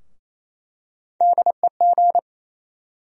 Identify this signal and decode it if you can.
Morse code, heard as a keyed tone, about 27 wpm, 700 Hz. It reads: DEG
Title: USA Call Sign Postfixes - 3 Letters 28wpm